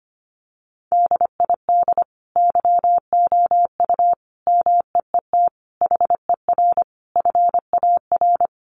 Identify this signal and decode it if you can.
Morse code, heard as a keyed tone, about 25 wpm, 690 Hz. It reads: DID YOU MEET HER FAR
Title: Sentences from Top 1100 Words 25wpm